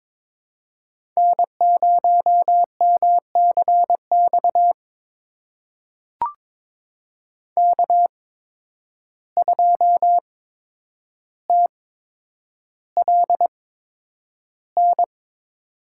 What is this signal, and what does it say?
Morse code, heard as a keyed tone, about 22 wpm, spaced as for 9 wpm, 700 Hz. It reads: N0MCX K2TLN
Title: USA Call Signs - 5 Characters - 8x Character Spacing - 22wpm